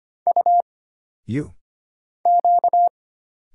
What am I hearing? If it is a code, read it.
Morse code, heard as a keyed tone, about 25 wpm, 700 Hz. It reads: U Q
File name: Single Character (Letter:Number:Punctuation:Prosign) - Mind-Melt - 25wpm